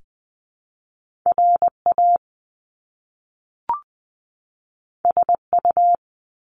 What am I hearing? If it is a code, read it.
Morse code, heard as a keyed tone, about 20 wpm, 700 Hz. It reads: RA SU
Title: Top Two Letter Combinations in Words 20wpm